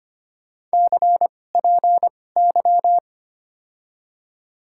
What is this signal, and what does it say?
Morse code, heard as a keyed tone, about 25 wpm, spaced as for 18 wpm, 700 Hz. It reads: CPY
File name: Top 90 QSO Elements - 2x Character Spacing - 25wpm